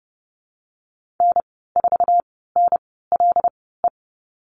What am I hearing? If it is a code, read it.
Morse code, heard as a keyed tone, about 30 wpm, spaced as for 17 wpm, 700 Hz. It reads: N4NLE
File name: Lesson 19g - Letter w - Call Signs 30wpm